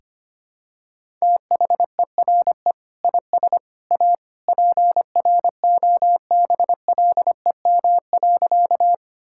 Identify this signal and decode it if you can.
Morse code, heard as a keyed tone, about 25 wpm, 700 Hz. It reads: THERE IS A PROBLEM.
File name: practice / 25wpm / Sentences with 3-Word Phrases - 25wpm